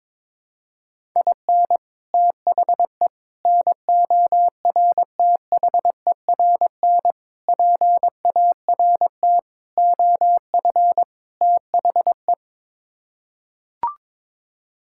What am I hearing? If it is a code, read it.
Morse code, heard as a keyed tone, about 22 wpm, 700 Hz. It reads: IN THE NORTHERN PART OF THE
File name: practice / 22wpm / Sets of 6 Words - 22wpm